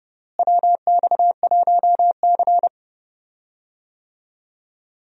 Morse code, heard as a keyed tone, about 31 wpm, 700 Hz. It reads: WX1C
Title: USA Call Signs - 4 Characters - Repeated 2x - 30wpm